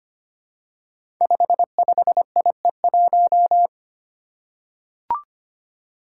Morse code, heard as a keyed tone, about 25 wpm, 700 Hz. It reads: 55IE1
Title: Five Challenging Letter-Number Combinations 25wpm